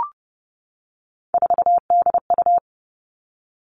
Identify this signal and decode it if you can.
Morse code, heard as a keyed tone, about 30 wpm, 700 Hz. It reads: 4DU
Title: Three Challenging Letter-Number Combinations 30wpm